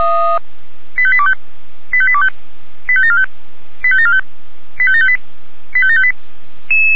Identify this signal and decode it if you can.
Five-tone selective call (ZVEI), heard as quick runs of five tones, tones of about 70 ms. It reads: 85825 85825 85745 85745 85758 85758
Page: /464/SDR/ZVEI/85726/